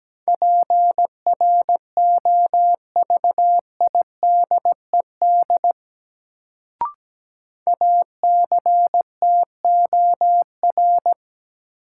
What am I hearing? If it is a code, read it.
Morse code, heard as a keyed tone, about 17 wpm, 700 Hz. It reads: PROVIDED ACTOR
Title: Sentences from Top 1500 Words - New Word Drill - 17wpm